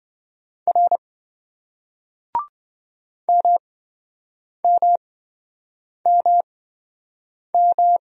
Morse code, heard as a keyed tone, about 28 wpm, spaced as for 7 wpm, 700 Hz. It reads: R MMMM